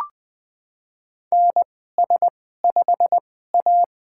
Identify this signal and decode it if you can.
Morse code, heard as a keyed tone, about 20 wpm, spaced as for 14 wpm, 700 Hz. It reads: NS5A
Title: USA Call Signs - 4 Characters - 2x Character Spacing - 20wpm